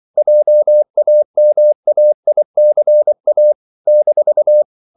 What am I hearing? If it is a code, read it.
Morse code, heard as a keyed tone, about 24 wpm, 600 Hz. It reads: JAMAICA -
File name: CW / Practice / NationalCapitals24